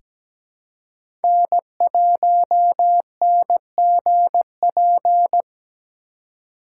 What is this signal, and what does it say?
Morse code, heard as a keyed tone, about 17 wpm, 700 Hz. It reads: N1NGP
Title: USA Call Signs - 5 Characters - 3x Character Spacing - 17wpm